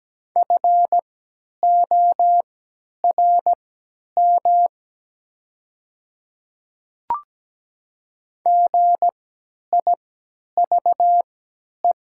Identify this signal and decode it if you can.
Morse code, heard as a keyed tone, about 17 wpm, spaced as for 10 wpm, 690 Hz. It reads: FORM GIVE
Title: 3x Character Spacing - Top 300 Words - 17wpm